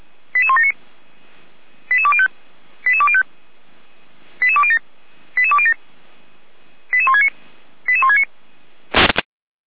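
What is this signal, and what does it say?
Five-tone selective call (ZVEI), heard as quick runs of five tones, tones of about 70 ms. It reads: 88179 88205 88205 88207 88207 88169 88169
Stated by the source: ZVEI five-tone call 88189 received on an SDR.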